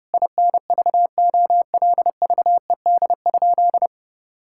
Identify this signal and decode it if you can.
Morse code, heard as a keyed tone, about 32 wpm, 700 Hz. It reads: INVOLVED?
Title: Sentences from Top 1000 Words - Review Part 5 - 30wpm